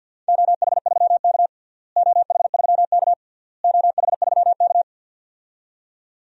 Morse code, heard as a keyed tone, about 52 wpm, 700 Hz. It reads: OH3K OH3K OH3K
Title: DXCC-100 Call Signs - 4 Characters - Repeated 3x - 50wpm